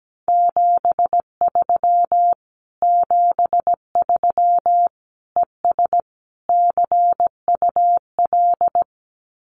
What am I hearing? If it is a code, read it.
Morse code, heard as a keyed tone, about 17 wpm, 700 Hz. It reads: 73 73 ES CUL
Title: Common QSO Phrases Encore 17wpm